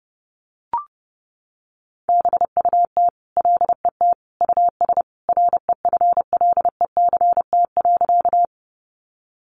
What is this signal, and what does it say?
Morse code, heard as a keyed tone, about 30 wpm, 700 Hz. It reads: BUT LET US REFLECT.
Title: Sentences from Top 1500 Words 30wpm